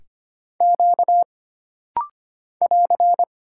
Morse code, heard as a keyed tone, about 25 wpm, 700 Hz. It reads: Q <AR>
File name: Single Character (Letter:Number:Punctuation:Prosign) - Rapid-Fire - 25wpm